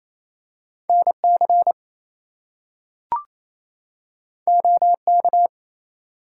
Morse code, heard as a keyed tone, about 28 wpm, 700 Hz. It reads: NC OK
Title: US States and Canadian Provinces 28wpm